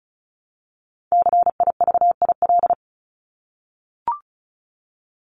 Morse code, heard as a keyed tone, about 34 wpm, 700 Hz. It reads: CIVIL
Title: Top 1751-2000 Words 35wpm